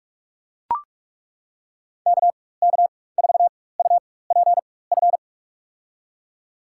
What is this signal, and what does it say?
Morse code, heard as a keyed tone, about 41 wpm, spaced as for 20 wpm, 700 Hz. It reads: KK4UPF
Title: USA Call Signs - 6 Characters - 4x Character Spacing - 45wpm